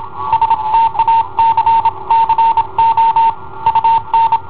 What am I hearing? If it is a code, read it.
Morse code, heard as a keyed tone, about 27 wpm, 910 Hz. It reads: STACCO UN
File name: i7ohp / cassetta i7ohp B